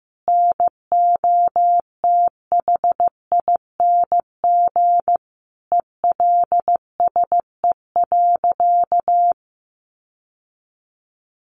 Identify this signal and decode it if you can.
Morse code, heard as a keyed tone, about 15 wpm, 700 Hz. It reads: NOTHING ELSE.
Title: Sentences from Top 1100 Words 15wpm